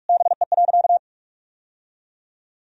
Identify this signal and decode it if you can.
Morse code, heard as a keyed tone, about 46 wpm, 700 Hz. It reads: BE.
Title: Sentences from Top 1000 Words - Review Part 3 - 45wpm